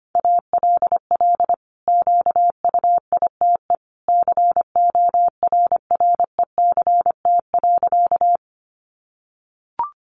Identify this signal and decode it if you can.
Morse code, heard as a keyed tone, about 25 wpm, 700 Hz. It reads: ALL QUITE CORRECT.
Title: Sentences from Top 700 Words 25wpm